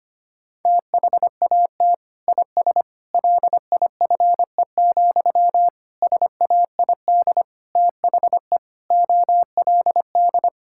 Morse code, heard as a keyed tone, about 25 wpm, 700 Hz. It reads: THAT IS LIFE, SAID THE OLD
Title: Sentences from Top 1000 Words - Review Part 6 - 25wpm